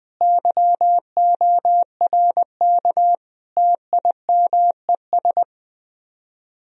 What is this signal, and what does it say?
Morse code, heard as a keyed tone, about 20 wpm, 700 Hz. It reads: YORK TIMES
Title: Speed-Racing - Sets of 5 Words 17wpm